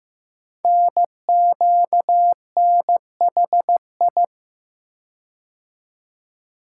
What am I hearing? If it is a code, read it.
Morse code, heard as a keyed tone, about 15 wpm, 700 Hz. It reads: NQNHI
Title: Code Groups - 5 Letters - 15wpm